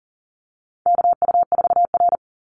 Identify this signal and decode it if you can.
Morse code, heard as a keyed tone, about 40 wpm, 700 Hz. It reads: KU4R
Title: NA SOTA Chaser Callsigns 40wpm